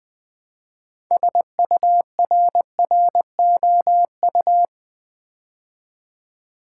Morse code, heard as a keyed tone, about 20 wpm, 700 Hz. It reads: SURROU
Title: Build A Word - 2 Letters At A Time - 8-Letter Words - 20wpm